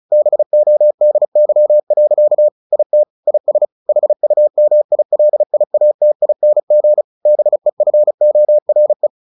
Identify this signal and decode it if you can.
Morse code, heard as a keyed tone, about 34 wpm, 600 Hz. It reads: BODY. IT IS HUMILIATING BEFORE
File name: the old man and the sea / the old man and the sea 1 35wpm